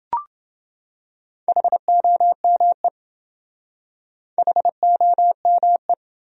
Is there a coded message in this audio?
Yes: Morse code from 1.5 to 5.9 s, heard as a keyed tone, about 28 wpm, 700 Hz. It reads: HOME HOME